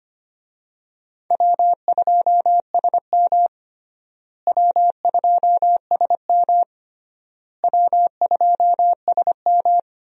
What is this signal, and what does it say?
Morse code, heard as a keyed tone, about 25 wpm, 700 Hz. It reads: W2SM W2SM W2SM
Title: USA Call Signs - 4 Characters - Repeated 3x - 25wpm